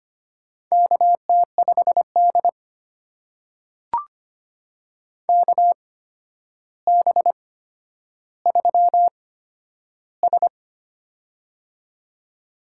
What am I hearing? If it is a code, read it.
Morse code, heard as a keyed tone, about 25 wpm, spaced as for 11 wpm, 700 Hz. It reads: KT5D KB3S
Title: USA Call Signs - 4 Characters - 8x Character Spacing - 25wpm